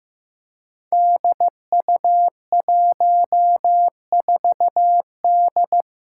Morse code, heard as a keyed tone, about 15 wpm, 700 Hz. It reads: DU14D
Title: Five Challenging Letter-Number Combinations 15wpm